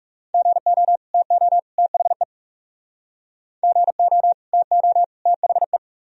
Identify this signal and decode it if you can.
Morse code, heard as a keyed tone, about 38 wpm, 700 Hz. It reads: GO TO THE GO TO THE